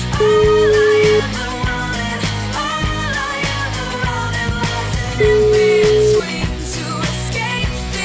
Ringback tone (Europe, 425 Hz), with noise.